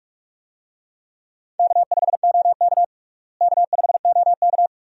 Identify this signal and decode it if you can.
Morse code, heard as a keyed tone, about 46 wpm, 700 Hz. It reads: K5OK K5OK
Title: USA Call Signs - 4 Characters - Repeated 2x - 45wpm